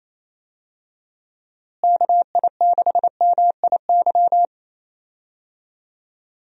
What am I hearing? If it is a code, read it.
Morse code, heard as a keyed tone, about 27 wpm, 700 Hz. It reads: KI6MIY
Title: USA Call Signs - 6 Characters - Repeated 2x - 28wpm